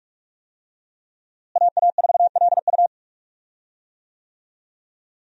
Morse code, heard as a keyed tone, about 43 wpm, 700 Hz. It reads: AA4LU
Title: USA Call Signs - 5 Characters - Repeated 3x - 45wpm